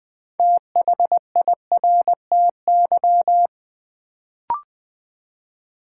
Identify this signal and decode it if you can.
Morse code, heard as a keyed tone, about 20 wpm, 700 Hz. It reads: THIRTY
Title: Build A Word - 3 Letters At A Time - 6-Letter Words - 20wpm